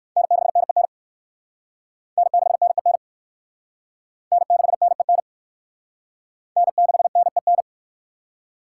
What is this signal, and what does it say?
Morse code, heard as a keyed tone, about 60 wpm, 700 Hz. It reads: N6NEN N6NEN N6NEN N6NEN